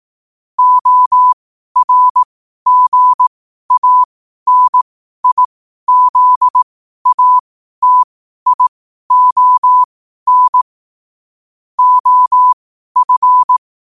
Morse code, heard as a keyed tone, about 18 wpm, spaced as for 12 wpm, 1000 Hz. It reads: ORGANIZATION OF
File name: qotdfiles / 12wpm